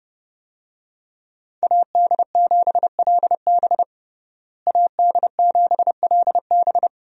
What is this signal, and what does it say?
Morse code, heard as a keyed tone, about 30 wpm, 700 Hz. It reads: AD7LB AD7LB
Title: USA Call Signs - 5 Characters - Repeated 3x - 30wpm